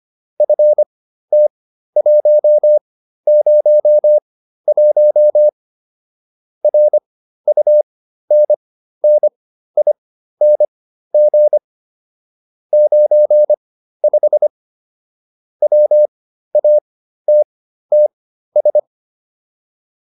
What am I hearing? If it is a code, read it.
Morse code, heard as a keyed tone, about 25 wpm, spaced as for 13 wpm, 600 Hz. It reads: FT101 RUNNING 95 WATTS